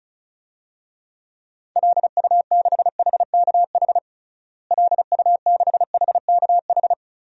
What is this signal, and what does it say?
Morse code, heard as a keyed tone, about 36 wpm, 700 Hz. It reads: LU6HKH LU6HKH
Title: DXCC-100 Call Signs - 6 Characters - Repeated 3x - 35wpm